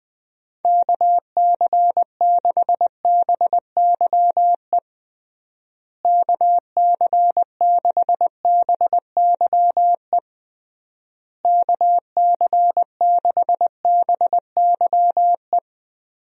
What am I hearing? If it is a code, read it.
Morse code, heard as a keyed tone, about 20 wpm, 700 Hz. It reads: KC6BYE KC6BYE KC6BYE
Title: USA Call Signs - 6 Characters - Repeated 3x - 20wpm